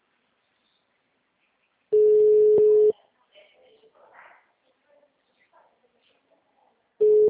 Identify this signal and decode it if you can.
Ringback tone (Europe, 425 Hz), with speech.
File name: out- -2007- - - .50819